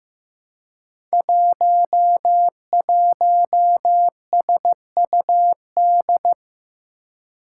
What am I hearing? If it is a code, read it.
Morse code, heard as a keyed tone, about 15 wpm, 700 Hz. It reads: 11SUD
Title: Five Challenging Letter-Number Combinations 15wpm